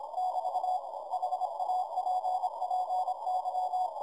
Morse code, heard as a keyed tone, about 26 wpm, 740 Hz, with noise. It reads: <BT> HAPPY
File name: 30mtr dir noisereduction EA6BH CW